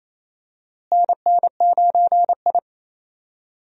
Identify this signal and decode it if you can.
Morse code, heard as a keyed tone, about 28 wpm, 700 Hz. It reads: NN9I
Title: USA Call Signs - 4 Characters - Repeated 2x - 28wpm